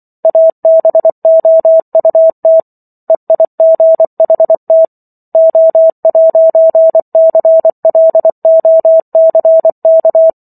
Morse code, heard as a keyed tone, about 24 wpm, 650 Hz. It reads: ABOUT EIGHT O'CLOCK